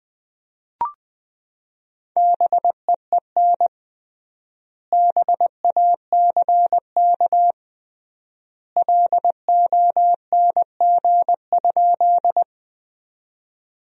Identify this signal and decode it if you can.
Morse code, heard as a keyed tone, about 20 wpm, 700 Hz. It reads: BEEN BACK LONG?
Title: 3x Word-Spacing - Sentences from Top 200 Words - 20wpm